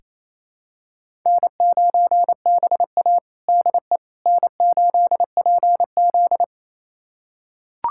Morse code, heard as a keyed tone, about 28 wpm, 700 Hz. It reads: N9BA DE N8PZ
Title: USA Callsign DE Callsign - 4 Characters - 28wpm